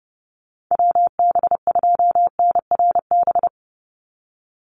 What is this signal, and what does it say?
Morse code, heard as a keyed tone, about 30 wpm, 700 Hz. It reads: WB2NRB
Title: USA Call Signs 30wpm